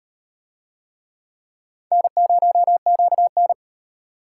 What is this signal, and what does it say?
Morse code, heard as a keyed tone, about 38 wpm, 700 Hz. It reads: N0QN